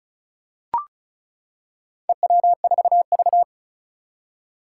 Morse code, heard as a keyed tone, about 36 wpm, 700 Hz. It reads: EW4V
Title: Four Challenging Letter-Number Combinations 35wpm